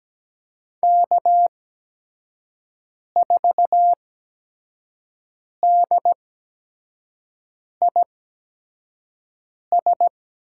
Morse code, heard as a keyed tone, about 17 wpm, spaced as for 5 wpm, 700 Hz. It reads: K4DIS
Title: USA Call Signs - 5 Characters - 8x Character Spacing - 17wpm